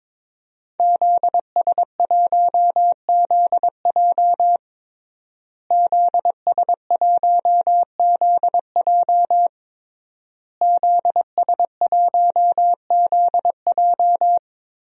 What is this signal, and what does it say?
Morse code, heard as a keyed tone, about 21 wpm, 700 Hz. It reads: ZS1ZJ ZS1ZJ ZS1ZJ
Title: DXCC-100 Call Signs - 5 Characters - Repeated 3x - 22wpm